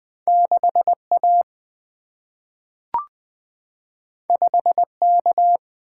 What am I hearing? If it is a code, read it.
Morse code, heard as a keyed tone, about 20 wpm, 700 Hz. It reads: 6A 5K
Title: Single-Letter Single-Number 20wpm